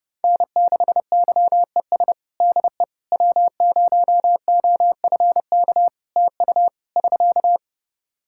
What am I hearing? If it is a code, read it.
Morse code, heard as a keyed tone, about 30 wpm, 700 Hz. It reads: N6YES DE W0OFK TU <SK>
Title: K1USN SST QSO Practice 30wpm